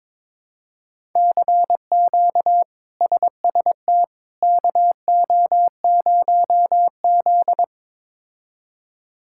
Morse code, heard as a keyed tone, about 21 wpm, 700 Hz. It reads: CQ SST KO0Z
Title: K1USN SST QSO Practice - 22wpm